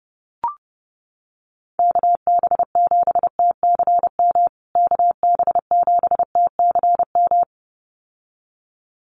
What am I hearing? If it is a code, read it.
Morse code, heard as a keyed tone, about 28 wpm, 700 Hz. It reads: KB7TCM KB7TCM
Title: USA Call Signs Repeated 2x 30wpm